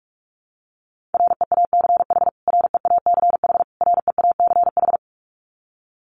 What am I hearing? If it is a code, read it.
Morse code, heard as a keyed tone, about 44 wpm, 700 Hz. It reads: REACH REACH REACH
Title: Top 500 Words Repeated 45wpm